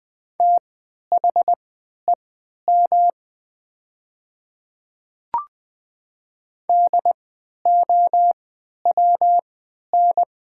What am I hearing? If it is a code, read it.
Morse code, heard as a keyed tone, about 20 wpm, spaced as for 11 wpm, 690 Hz. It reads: THEM DOWN
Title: 3x Character Spacing - Top 100 Words - 20wpm